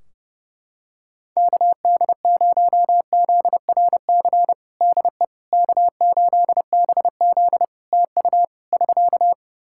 Morse code, heard as a keyed tone, about 31 wpm, 700 Hz. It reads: KD0ZRC DE K8BZ TU <SK>
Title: Practice QSO - Exchange 4 - Ending - 30wpm